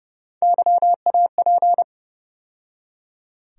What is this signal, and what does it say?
Morse code, heard as a keyed tone, about 30 wpm, 700 Hz. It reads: YAP
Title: USA Call Sign Postfixes - 3 Letters 30wpm